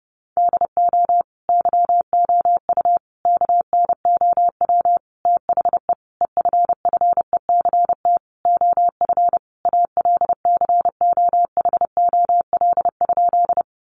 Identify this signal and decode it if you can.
Morse code, heard as a keyed tone, about 30 wpm, 700 Hz. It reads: DO YOU KNOW THE EFFECT OF ALCOHOL?
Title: Sentences from Top 1400 Words 30wpm